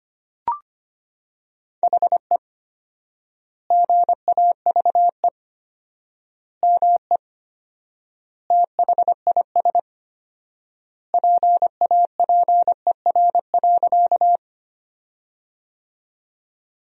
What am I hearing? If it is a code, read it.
Morse code, heard as a keyed tone, about 25 wpm, 700 Hz. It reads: HE GAVE ME THIS PAPER.
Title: 4x Word-Spacing - Sentences from Top 500 Words - 25wpm